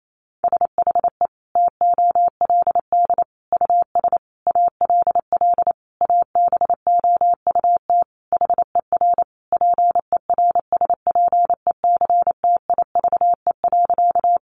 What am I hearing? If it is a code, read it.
Morse code, heard as a keyed tone, about 28 wpm, 700 Hz. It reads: SHE TOLD US ALL ABOUT HER PERSPECTIVE.
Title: Sentences from Top 1200 Words 28wpm